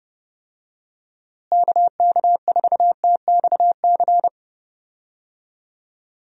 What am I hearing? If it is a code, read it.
Morse code, heard as a keyed tone, about 30 wpm, 700 Hz. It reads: KK4TXC
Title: USA Call Signs - 6 Characters - Repeated 3x - 30wpm